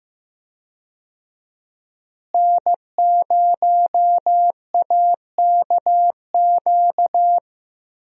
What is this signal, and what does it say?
Morse code, heard as a keyed tone, about 15 wpm, 700 Hz. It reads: N0AKQ
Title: USA Call Signs - 5 Characters - Repeated 3x - 15wpm